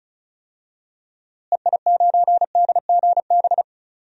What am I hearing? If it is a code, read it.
Morse code, heard as a keyed tone, about 36 wpm, 700 Hz. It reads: EI9DGB
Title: DXCC-100 Call Signs - 6 Characters - Repeated 2x - 35wpm